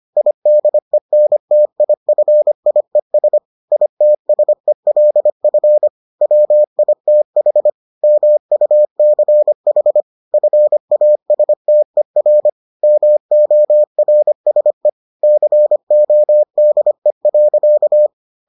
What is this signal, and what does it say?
Morse code, heard as a keyed tone, about 25 wpm, 600 Hz. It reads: IDENTIFIES ITSELF WITH MUCH FASTER MORSE CODE.